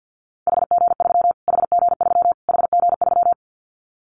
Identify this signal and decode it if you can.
Morse code, heard as a keyed tone, about 48 wpm, 700 Hz. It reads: HG3 HG3 HG3
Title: Callsign Prefixes for Top 25 DXCC - Repeated 3x - 50wpm